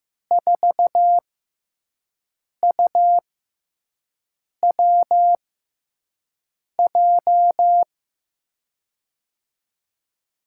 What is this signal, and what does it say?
Morse code, heard as a keyed tone, about 15 wpm, spaced as for 5 wpm, 700 Hz. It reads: 4UWJ
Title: USA Call Signs - 6 Characters - 6x Character Spacing - 15wpm